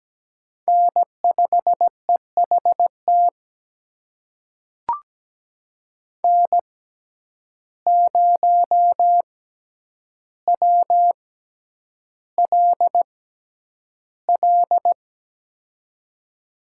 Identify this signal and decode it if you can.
Morse code, heard as a keyed tone, about 17 wpm, spaced as for 9 wpm, 700 Hz. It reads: N5EHT N0WLL
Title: USA Call Signs - 5 Characters - 6x Character Spacing - 17wpm